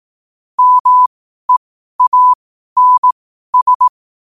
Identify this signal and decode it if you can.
Morse code, heard as a keyed tone, about 18 wpm, spaced as for 12 wpm, 1000 Hz. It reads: MEANS
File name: qotdfiles / 12wpm